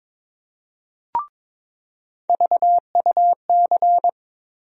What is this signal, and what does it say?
Morse code, heard as a keyed tone, about 22 wpm, 700 Hz. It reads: VUC
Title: USA Call Sign Postfixes - 3 Letters 22wpm